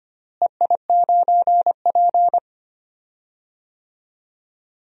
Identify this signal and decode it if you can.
Morse code, heard as a keyed tone, about 25 wpm, 700 Hz. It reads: EI9P
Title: DXCC-100 Call Signs - 4 Characters - Repeated 3x - 25wpm